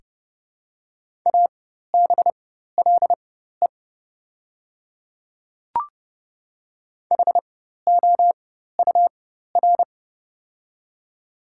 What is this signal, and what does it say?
Morse code, heard as a keyed tone, about 30 wpm, spaced as for 14 wpm, 700 Hz. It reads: ABLE HOUR
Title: 4x Character Spacing - Top 500 Words - 30wpm